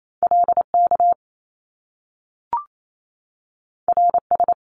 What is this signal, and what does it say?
Morse code, heard as a keyed tone, about 28 wpm, 700 Hz. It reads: LK RS
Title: Top Two Letter Combinations in Words 28wpm